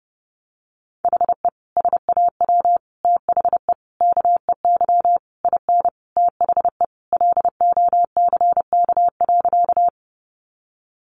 Morse code, heard as a keyed tone, about 30 wpm, 700 Hz. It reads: HE SAW THE KEY IN THE LOCK.
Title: Sentences from Top 1300 Words 30wpm